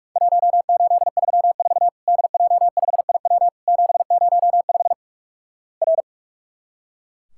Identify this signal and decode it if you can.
Morse code, heard as a keyed tone, about 44 wpm, 700 Hz. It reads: 1834 DJ5IW 705 R
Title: Worked All Europe DX Contest - QTC Exchange - 45wpm